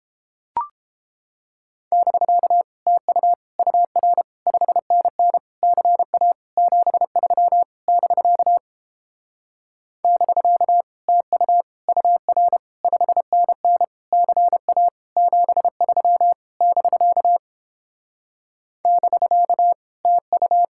Morse code, heard as a keyed tone, about 29 wpm, 700 Hz. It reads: <BK> TU UR 5NN CA 73 <BK> <BK> TU UR 5NN CA 73 <BK> <BK> TU